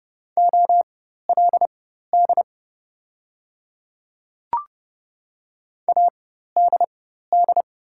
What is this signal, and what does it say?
Morse code, heard as a keyed tone, about 30 wpm, spaced as for 14 wpm, 700 Hz. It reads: OLD ADD
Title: 4x Character Spacing - Top 200 Words - 30wpm